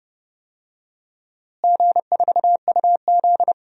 Morse code, heard as a keyed tone, about 30 wpm, 700 Hz. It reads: G4UZ
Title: DXCC-100 Call Signs - 4 Characters - Repeated 2x - 30wpm